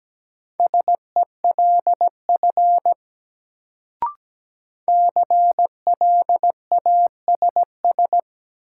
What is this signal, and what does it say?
Morse code, heard as a keyed tone, about 17 wpm, 700 Hz. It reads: SELF CLASS
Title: Sentences from Top 1400 Words - New Word Drill Alt - 17wpm